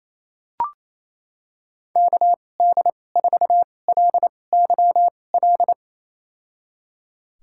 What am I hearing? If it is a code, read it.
Morse code, heard as a keyed tone, about 27 wpm, spaced as for 20 wpm, 700 Hz. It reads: KD4LYL
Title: USA Call Signs - 2x Character Spacing - 28wpm